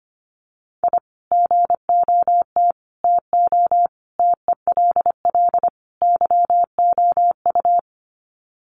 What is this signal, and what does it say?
Morse code, heard as a keyed tone, about 25 wpm, 700 Hz. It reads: I GOT TO TELL YOU
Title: Sets of 5 Words Encore2 25wpm